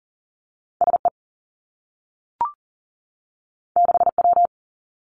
Morse code, heard as a keyed tone, about 40 wpm, 700 Hz. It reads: S E 6 W
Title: Two Challenging Letter-Number Combinations 40wpm